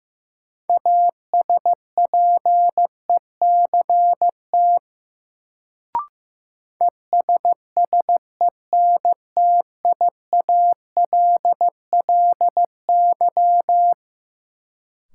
Morse code, heard as a keyed tone, about 15 wpm, 700 Hz. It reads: ASPECT ESSENTIALLY
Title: Sentences from Top 1300 Words - New Word Drill - 15wpm